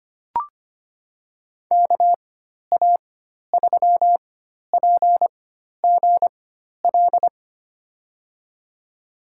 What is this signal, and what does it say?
Morse code, heard as a keyed tone, about 25 wpm, spaced as for 12 wpm, 700 Hz. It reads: KA3PGL
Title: USA Call Signs - 6 Characters - 4x Character Spacing - 25wpm